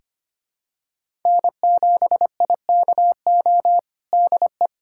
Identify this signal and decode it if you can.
Morse code, heard as a keyed tone, about 25 wpm, 700 Hz. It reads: N7IKO DE
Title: USA Callsign DE Callsign 25wpm